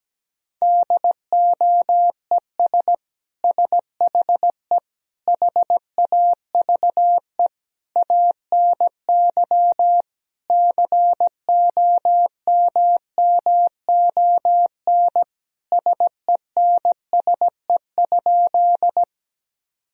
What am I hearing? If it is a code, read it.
Morse code, heard as a keyed tone, about 17 wpm, 700 Hz. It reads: DOES SHE HAVE ANY COMMON SENSE?
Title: Sentences from Top 1000 Words - Review Part 4 - 17wpm